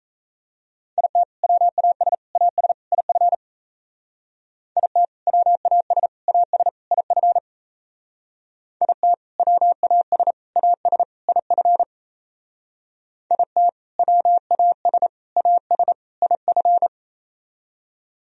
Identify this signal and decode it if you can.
Morse code, heard as a keyed tone, about 34 wpm, 700 Hz. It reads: IT WAS AS IF IT WAS AS IF IT WAS AS IF IT WAS AS IF
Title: Speed-Racing - Sets of 4 Words 28wpm